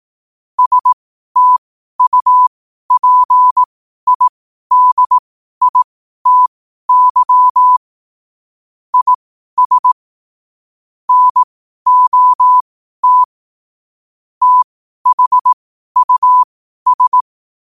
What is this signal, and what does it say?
Morse code, heard as a keyed tone, about 18 wpm, spaced as for 12 wpm, 1000 Hz. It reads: STUPIDITY IS NOT THUS
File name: qotdfiles / 12wpm